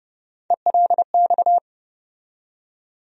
Morse code, heard as a keyed tone, about 31 wpm, 700 Hz. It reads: ELX
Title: USA Call Sign Postfixes - 3 Letters 30wpm